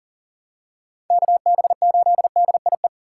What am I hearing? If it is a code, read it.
Morse code, heard as a keyed tone, about 40 wpm, 700 Hz. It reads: KB8DIE
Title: USA Call Signs - 6 Characters - 2x Character Spacing - 40wpm